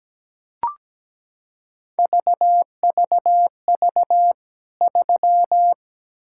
Morse code, heard as a keyed tone, about 17 wpm, 700 Hz. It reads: VVV 3
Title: K1USN SST QSO Practice 17wpm